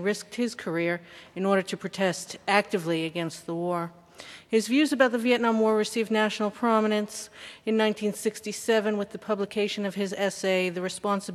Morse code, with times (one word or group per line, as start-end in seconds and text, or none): none